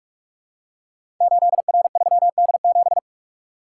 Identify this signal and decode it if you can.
Morse code, heard as a keyed tone, about 44 wpm, 700 Hz. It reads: 8R3D7
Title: Code Groups - 5 Letter:Numbers - 45wpm